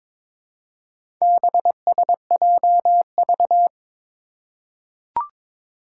Morse code, heard as a keyed tone, about 22 wpm, 700 Hz. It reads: BSJV
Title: Four Challenging Letter-Number Combinations 22wpm